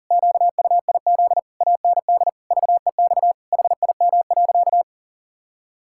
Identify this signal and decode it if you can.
Morse code, heard as a keyed tone, about 40 wpm, 700 Hz. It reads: QUIZ AND VEX HIM.
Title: Pangram Challenge v2 - 40wpm